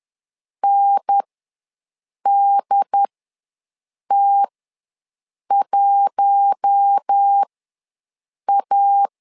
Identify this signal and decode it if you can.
Morse code, heard as a keyed tone, about 10 wpm, spaced as for 6 wpm, 800 Hz. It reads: NDT1A
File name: LEZIONI / 4 Q1 ws 10wpm 2538